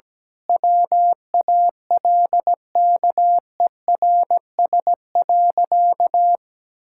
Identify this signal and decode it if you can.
Morse code, heard as a keyed tone, about 17 wpm, 700 Hz. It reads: WALKERS.
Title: Pangram Challenge v2 - 17wpm